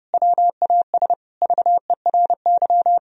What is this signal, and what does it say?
Morse code, heard as a keyed tone, about 30 wpm, 700 Hz. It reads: WAS VERY
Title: Sentences from Top 1000 Words - Review Part 4 - 30wpm